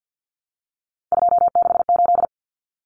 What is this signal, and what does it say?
Morse code, heard as a keyed tone, about 50 wpm, 700 Hz. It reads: 268
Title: Common Cut Numbers - 3-Digit 50wpm